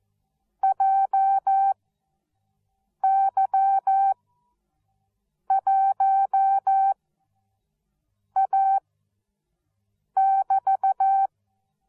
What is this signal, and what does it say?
Morse code, heard as a keyed tone, about 14 wpm, spaced as for 5 wpm, 780 Hz. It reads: JY1A<BT>
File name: Onlinelehrgang / morsen / DJ4UF Morsen13